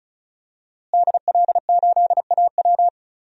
Morse code, heard as a keyed tone, about 36 wpm, 700 Hz. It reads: DL8AW
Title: DXCC-100 Call Signs - 5 Characters - Repeated 2x - 35wpm